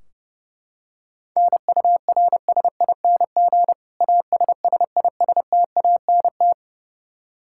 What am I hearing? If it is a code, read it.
Morse code, heard as a keyed tone, about 32 wpm, 700 Hz. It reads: NURSING ASSISTANT
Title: Practice QSO - Exchange 3 - More Info - 30wpm